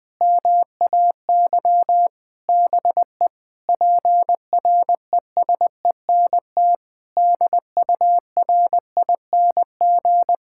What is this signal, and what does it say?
Morse code, heard as a keyed tone, about 21 wpm, 700 Hz. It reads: MAY BE PRESENT DURING
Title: Sentences from Top 1600 Words 20wpm